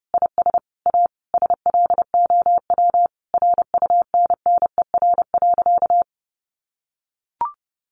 Morse code, heard as a keyed tone, about 30 wpm, 700 Hz. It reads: IS A SLOW RUNNER.
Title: Additional Easy Phrases 30wpm